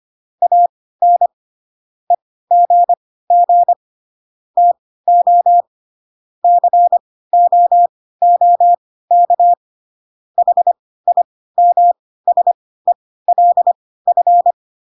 Morse code, heard as a keyed tone, about 25 wpm, spaced as for 16 wpm, 700 Hz. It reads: AN EGG TO COOK HIMSELF